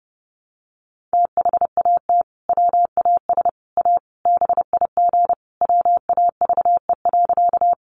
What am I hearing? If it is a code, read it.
Morse code, heard as a keyed tone, about 30 wpm, 700 Hz. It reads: THAT WAS A BIG WAVE.
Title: Sentences from Top 1400 Words 30wpm